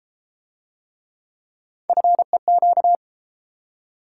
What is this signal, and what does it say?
Morse code, heard as a keyed tone, about 33 wpm, 700 Hz. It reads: FEQ